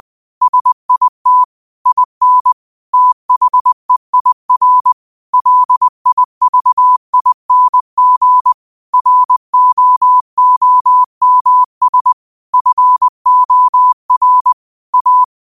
Morse code, heard as a keyed tone, about 20 wpm, 1000 Hz. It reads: SIT IN THEIR LIVING ROOMS FOR A